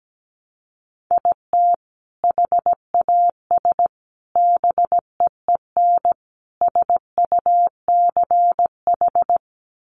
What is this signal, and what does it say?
Morse code, heard as a keyed tone, about 17 wpm, 700 Hz. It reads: IT HAS BEEN SUCH
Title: Sentences from Top 1100 Words 17wpm